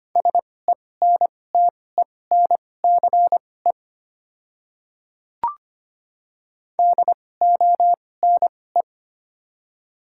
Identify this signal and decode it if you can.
Morse code, heard as a keyed tone, about 25 wpm, spaced as for 18 wpm, 700 Hz. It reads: SENTENCE DONE
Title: 2x Character Spacing - Top 500 Words - 25wpm